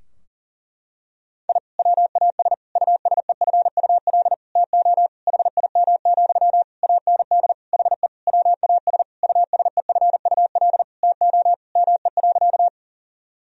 Morse code, heard as a keyed tone, about 40 wpm, 700 Hz. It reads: I WAS USEFUL TO HIM, AND HE WAS USEFUL TO ME.
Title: Sentences from Top 1000 Words - Review Part 4 - 40wpm